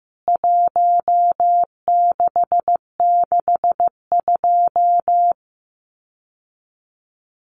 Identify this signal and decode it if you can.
Morse code, heard as a keyed tone, about 15 wpm, 700 Hz. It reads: 1662
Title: CWT Exchange 15wpm